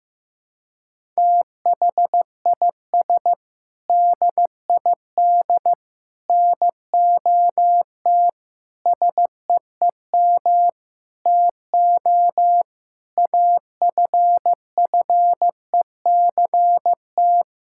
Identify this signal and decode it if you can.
Morse code, heard as a keyed tone, about 15 wpm, 700 Hz. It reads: THIS DID NOT SEEM TO AFFECT
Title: Sentences from Top 1000 Words - Review Part 6 - 15wpm